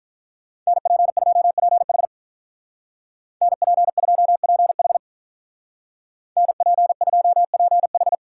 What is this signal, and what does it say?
Morse code, heard as a keyed tone, about 47 wpm, 700 Hz. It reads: NP2PH NP2PH NP2PH